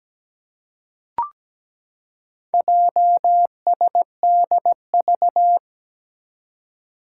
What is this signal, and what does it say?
Morse code, heard as a keyed tone, about 17 wpm, 700 Hz. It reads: JSDV
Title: Four Challenging Letter-Number Combinations 17wpm